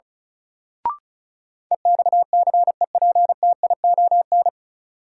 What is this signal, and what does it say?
Morse code, heard as a keyed tone, about 34 wpm, 700 Hz. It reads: EXCEPTION
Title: Sets of 4 Words - New Word Drill - 35wpm